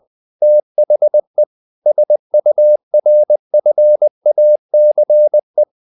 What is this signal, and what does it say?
Morse code, heard as a keyed tone, about 20 wpm, 600 Hz. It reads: THE SURFACE
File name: the old man and the sea / the old man and the sea part 1 20wpm